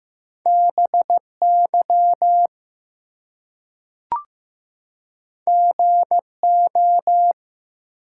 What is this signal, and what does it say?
Morse code, heard as a keyed tone, about 15 wpm, 700 Hz. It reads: BY GO
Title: Two Letter Words - 15wpm